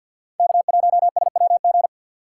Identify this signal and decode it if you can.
Morse code, heard as a keyed tone, about 47 wpm, 700 Hz. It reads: K1SWG
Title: USA Call Signs - 5 Characters - 3x Character Spacing - 50wpm